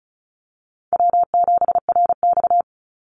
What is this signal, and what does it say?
Morse code, heard as a keyed tone, about 34 wpm, 700 Hz. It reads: W7RX
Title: Call Signs - 35wpm